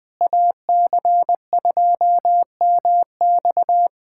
Morse code, heard as a keyed tone, about 20 wpm, 700 Hz. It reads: AC2MX
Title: USA Callsign DE Callsign - 5 Characters - 20wpm